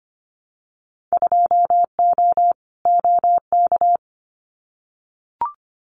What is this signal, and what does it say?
Morse code, heard as a keyed tone, about 25 wpm, 700 Hz. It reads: 2O OK
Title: Winter Field Day Exchanges Encore 25wpm